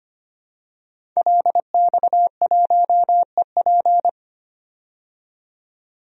Morse code, heard as a keyed tone, about 25 wpm, 700 Hz. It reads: LX1EP
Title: DXCC-100 Call Signs - 5 Characters - 25wpm